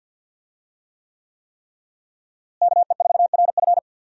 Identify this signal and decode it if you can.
Morse code, heard as a keyed tone, about 50 wpm, 700 Hz. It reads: KE4RF
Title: USA Call Signs - 5 Characters - Repeated 3x - 50wpm